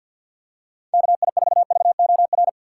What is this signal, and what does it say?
Morse code, heard as a keyed tone, about 52 wpm, 700 Hz. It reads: KI4VOR
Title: USA Call Signs - 3x Character Spacing - 50wpm